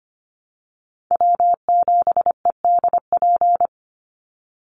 Morse code, heard as a keyed tone, about 25 wpm, 700 Hz. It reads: W7EDP
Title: USA Call Signs 25wpm